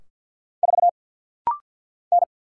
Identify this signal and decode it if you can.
Morse code, heard as a keyed tone, about 50 wpm, 700 Hz. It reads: 4 N
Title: Single Letter-Number Alt - Rapid-Fire 50wpm